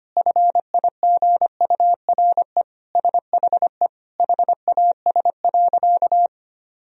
Morse code, heard as a keyed tone, about 25 wpm, 700 Hz. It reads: FIGURE SHE HAS.
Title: Sentences from Top 1000 Words - Review Part 2 - 25wpm